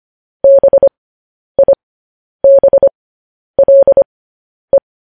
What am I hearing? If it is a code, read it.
Morse code, heard as a keyed tone, about 25 wpm, spaced as for 10 wpm, 550 Hz. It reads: BIBLE
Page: news-headlines-in-morse-code-at-10-wpm